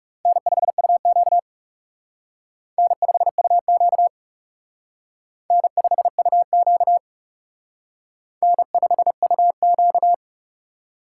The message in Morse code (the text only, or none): N5UQ N5UQ N5UQ N5UQ